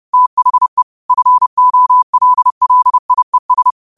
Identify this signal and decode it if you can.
Morse code, heard as a keyed tone, about 30 wpm, 1000 Hz. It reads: THE FOLLIES